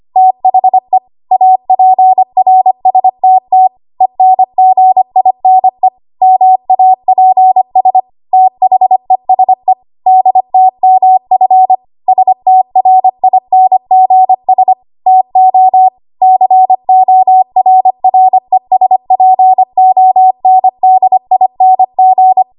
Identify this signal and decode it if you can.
Morse code, heard as a keyed tone, about 25 wpm, 750 Hz. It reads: THE APRSTT ENGINE MAPS THESE DTMF STRINGS TO CORRESPONDING
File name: Morse / 25WPM